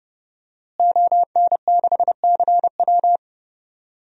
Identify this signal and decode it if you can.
Morse code, heard as a keyed tone, about 30 wpm, 700 Hz. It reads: ON6CW
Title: DXCC-100 Call Signs - 5 Characters - Repeated 3x - 30wpm